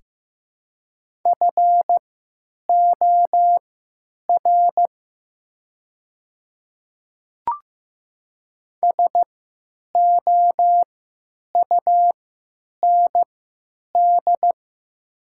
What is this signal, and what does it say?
Morse code, heard as a keyed tone, about 15 wpm, spaced as for 9 wpm, 700 Hz. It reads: FOR SOUND
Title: 3x Character Spacing - Top 300 Words - 15wpm